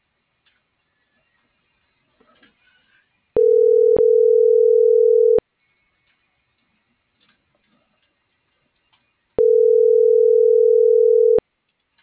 Ringback tone (North America), 440 + 480 Hz.